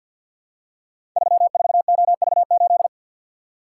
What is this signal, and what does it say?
Morse code, heard as a keyed tone, about 51 wpm, 700 Hz. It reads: 34OV8
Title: Code Groups - 5 Letter:Numbers - 50wpm